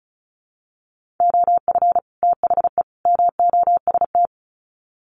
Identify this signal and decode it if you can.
Morse code, heard as a keyed tone, about 34 wpm, 700 Hz. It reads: OF THE MOST
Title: Sets of 3 Words 35wpm